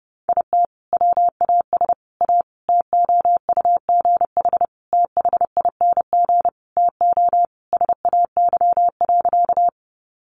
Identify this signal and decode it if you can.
Morse code, heard as a keyed tone, about 30 wpm, 700 Hz. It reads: IT WAS A TOUGH THING TO SAY.
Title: Sentences from Top 1200 Words 30wpm